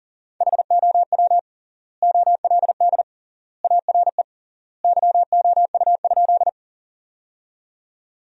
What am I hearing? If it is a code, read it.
Morse code, heard as a keyed tone, about 40 wpm, 700 Hz. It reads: HOW OLD ARE YOU?
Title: 3x Word-Spacing - Sentences from Top 200 Words - 40wpm